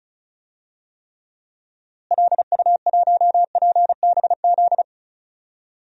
Morse code, heard as a keyed tone, about 34 wpm, 700 Hz. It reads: LU1PBZ
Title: DXCC-100 Call Signs - 6 Characters - Repeated 2x - 35wpm